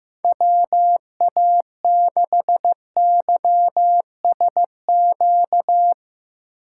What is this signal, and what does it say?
Morse code, heard as a keyed tone, about 15 wpm, 700 Hz. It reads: WA6YSQ
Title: Speed Racing - USA 2x3 Call Signs 15wpm